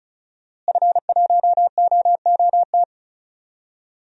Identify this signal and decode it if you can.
Morse code, heard as a keyed tone, about 36 wpm, 700 Hz. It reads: F1OOT
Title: DXCC-100 Call Signs - 5 Characters - Repeated 2x - 35wpm